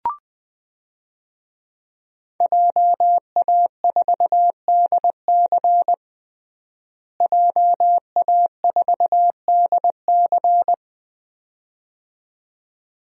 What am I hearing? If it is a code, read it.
Morse code, heard as a keyed tone, about 21 wpm, 700 Hz. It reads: JA4DC JA4DC
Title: DXCC-100 Call Signs - 5 Characters - Repeated 2x - 20wpm